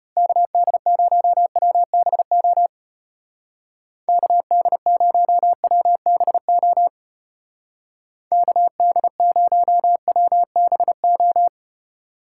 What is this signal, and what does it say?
Morse code, heard as a keyed tone, about 32 wpm, 700 Hz. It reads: KD0WBO KD0WBO KD0WBO